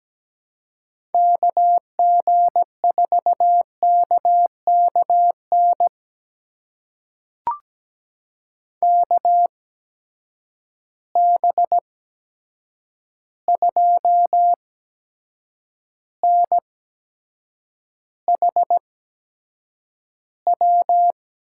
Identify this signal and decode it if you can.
Morse code, heard as a keyed tone, about 17 wpm, spaced as for 7 wpm, 700 Hz. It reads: KG4KKN KB2NHW
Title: USA Call Signs - 6 Characters - 8x Character Spacing - 17wpm